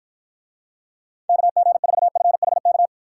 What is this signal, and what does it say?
Morse code, heard as a keyed tone, about 53 wpm, 700 Hz. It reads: KG4FHK